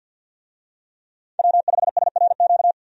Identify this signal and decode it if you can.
Morse code, heard as a keyed tone, about 51 wpm, 700 Hz. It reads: W5SRQ
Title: USA Call Signs - 5 Characters - 50wpm